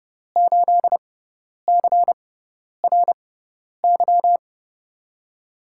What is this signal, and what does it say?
Morse code, heard as a keyed tone, about 30 wpm, spaced as for 10 wpm, 700 Hz. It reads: 8CRY
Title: USA Call Signs - 6x Character Spacing - 30wpm